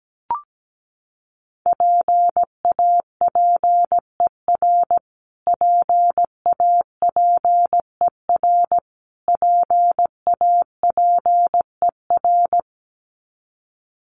Morse code, heard as a keyed tone, about 17 wpm, 700 Hz. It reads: PAPER PAPER PAPER
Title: Top 500 Words Repeated 17wpm